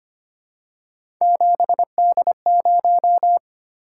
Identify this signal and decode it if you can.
Morse code, heard as a keyed tone, about 25 wpm, 700 Hz. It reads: 7D0
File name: Three Letter-Number Combinations - 25wpm